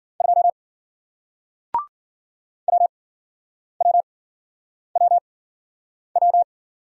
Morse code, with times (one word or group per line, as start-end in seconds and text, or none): 0.2-0.5 2
2.7-6.4 WWWW